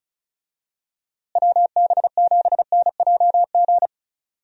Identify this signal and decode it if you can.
Morse code, heard as a keyed tone, about 34 wpm, 700 Hz. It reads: WB7NJG
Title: USA Call Signs - 6 Characters - Repeated 3x - 35wpm